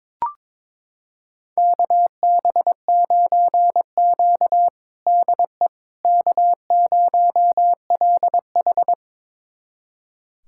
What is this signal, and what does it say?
Morse code, heard as a keyed tone, about 22 wpm, 700 Hz. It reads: KB9Q DE K0LH
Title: USA Callsign DE Callsign - 4 Characters - 22wpm